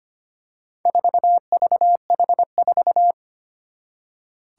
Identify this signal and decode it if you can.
Morse code, heard as a keyed tone, about 25 wpm, 700 Hz. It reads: 4VH4
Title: Four Challenging Letter-Number Combinations 25wpm